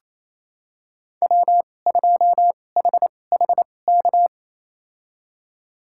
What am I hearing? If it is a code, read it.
Morse code, heard as a keyed tone, about 28 wpm, spaced as for 20 wpm, 700 Hz. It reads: W2HHK
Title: USA Call Signs - 5 Characters - 2x Character Spacing - 28wpm